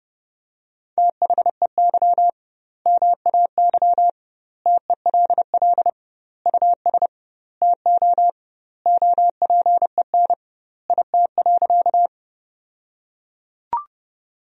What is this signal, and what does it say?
Morse code, heard as a keyed tone, about 32 wpm, 700 Hz. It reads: THEY MAY TELL US TO OPEN IT.
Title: 2x Word-Spacing - Sentences from Top 300 Words - 30wpm